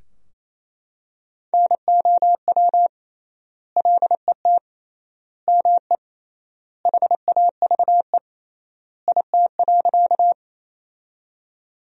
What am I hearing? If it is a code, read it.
Morse code, heard as a keyed tone, about 28 wpm, 700 Hz. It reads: NOW LET ME HAVE IT.
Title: 3x Word-Spacing - Sentences from Top 300 Words - 28wpm